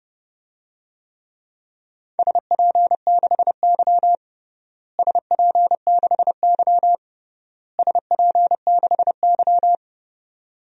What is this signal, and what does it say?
Morse code, heard as a keyed tone, about 32 wpm, 700 Hz. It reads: SP6Y SP6Y SP6Y
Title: DXCC-100 Call Signs - 4 Characters - Repeated 3x - 30wpm